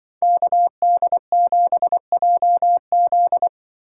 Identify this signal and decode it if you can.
Morse code, heard as a keyed tone, about 25 wpm, 700 Hz. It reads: KD7JZ
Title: Speed Racing - USA Call Signs 20wpm